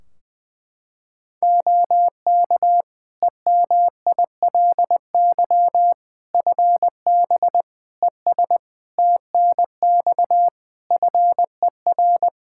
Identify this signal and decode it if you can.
Morse code, heard as a keyed tone, about 21 wpm, 700 Hz. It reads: OK EMILY FB ES TNX FER
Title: Practice QSO - Exchange 2 - Info - 20wpm